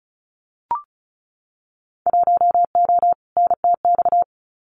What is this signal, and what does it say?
Morse code, heard as a keyed tone, about 34 wpm, 690 Hz. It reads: 1O NTX
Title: Winter Field Day Exchanges 35wpm